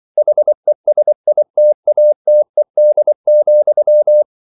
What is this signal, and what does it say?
Morse code, heard as a keyed tone, about 24 wpm, 600 Hz. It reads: HESITATED,